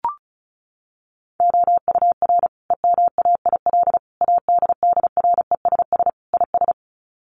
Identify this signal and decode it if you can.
Morse code, heard as a keyed tone, about 34 wpm, 700 Hz. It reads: OUR EMAIL ADDRESS IS
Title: Sets of 5 Words Encore2 35wpm